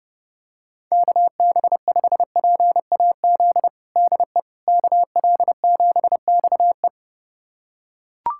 Morse code, heard as a keyed tone, about 31 wpm, 700 Hz. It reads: KB5PAZ DE KL7XE
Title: USA Callsign DE Callsign 30wpm